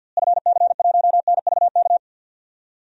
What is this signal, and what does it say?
Morse code, heard as a keyed tone, about 52 wpm, 700 Hz. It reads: FC1NVK
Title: DXCC-100 Call Signs - 6 Characters - Repeated 3x - 50wpm